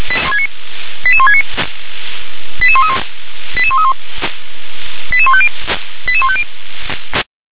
Five-tone selective call (ZVEI), heard as quick runs of five tones, tones of about 70 ms. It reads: 88169 88131 88131 88150 88150